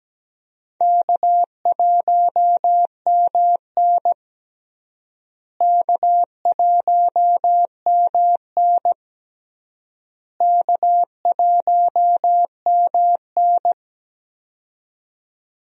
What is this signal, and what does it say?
Morse code, heard as a keyed tone, about 17 wpm, 700 Hz. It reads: K1MN K1MN K1MN
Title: USA Call Signs - 4 Characters - Repeated 3x - 17wpm